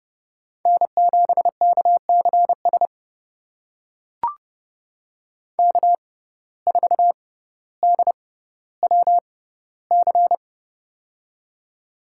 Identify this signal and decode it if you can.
Morse code, heard as a keyed tone, about 32 wpm, spaced as for 16 wpm, 700 Hz. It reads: N7KCS K4DWC
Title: USA Call Signs - 5 Characters - 6x Character Spacing - 30wpm